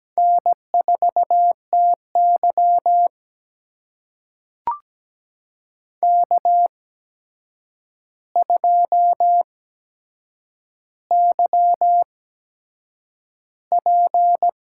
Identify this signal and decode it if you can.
Morse code, heard as a keyed tone, about 17 wpm, spaced as for 7 wpm, 700 Hz. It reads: N4TY K2YP
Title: USA Call Signs - 4 Characters - 8x Character Spacing - 17wpm